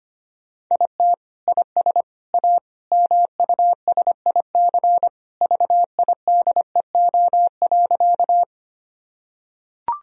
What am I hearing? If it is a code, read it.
Morse code, heard as a keyed tone, about 25 wpm, 700 Hz. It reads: IT IS A MUSIC VIDEO.
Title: Sentences from Top 1000 Words - Review Part 4 - 25wpm